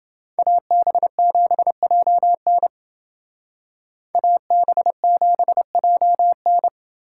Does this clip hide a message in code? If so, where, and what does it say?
Yes: Morse code from 0.4 to 6.7 s, heard as a keyed tone, about 28 wpm, 700 Hz. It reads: AB7JN AB7JN